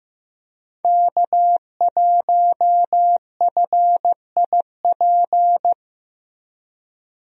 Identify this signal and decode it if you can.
Morse code, heard as a keyed tone, about 15 wpm, 700 Hz. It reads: K1FIP
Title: USA Call Signs - 5 Characters - Repeated 3x - 15wpm